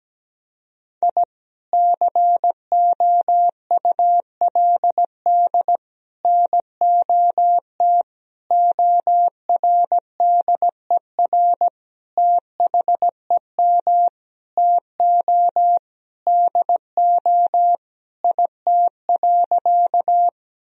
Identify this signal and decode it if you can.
Morse code, heard as a keyed tone, about 17 wpm, 700 Hz. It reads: I COULD NOT ORDER THEM TO DO IT.
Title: Sentences from Top 1000 Words - Review Part 2 - 17wpm